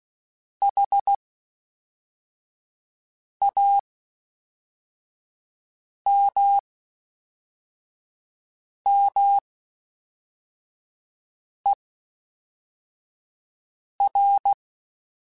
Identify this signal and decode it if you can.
Morse code, heard as a keyed tone, about 16 wpm, spaced as for 4 wpm, 800 Hz. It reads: HAMMER
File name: Misc / Misc-mp3s / WordPairs sp @5